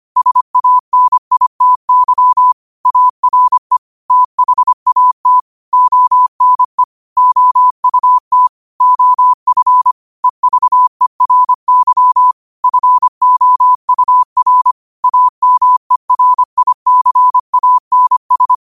Morse code, heard as a keyed tone, about 25 wpm, 1000 Hz. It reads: SANITY ARE THAT ONE OUT OF EVERY FOUR AMERICANS
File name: qotdfiles / 25wpm